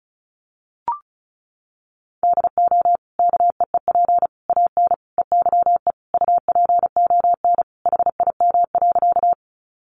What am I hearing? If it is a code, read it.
Morse code, heard as a keyed tone, about 34 wpm, 700 Hz. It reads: DO KEEP AN EYE UPON HIM.
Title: Sentences from Top 1100 Words 35wpm